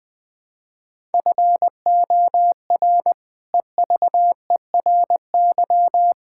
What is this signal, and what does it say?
Morse code, heard as a keyed tone, about 19 wpm, 700 Hz. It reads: FOR EVERY
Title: Sentences with 8-Word Phrases - Spoken Beginning - 20wpm